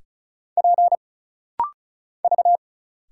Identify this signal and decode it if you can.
Morse code, heard as a keyed tone, about 34 wpm, 700 Hz. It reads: P V
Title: Single Letters - Rapid-Fire 35wpm